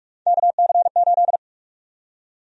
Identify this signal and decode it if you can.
Morse code, heard as a keyed tone, about 47 wpm, 700 Hz. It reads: KC8
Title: USA Call Sign Prefixes - 45wpm